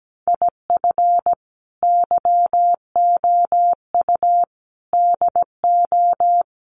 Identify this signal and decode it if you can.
Morse code, heard as a keyed tone, about 17 wpm, 700 Hz. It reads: IF YOU DO
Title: Sets of 5 Words 17wpm